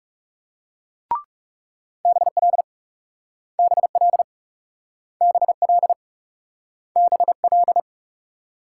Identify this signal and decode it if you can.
Morse code, heard as a keyed tone, about 39 wpm, 700 Hz. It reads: BL BL BL BL